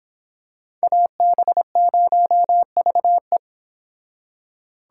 Morse code, heard as a keyed tone, about 25 wpm, 700 Hz. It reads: AB0VE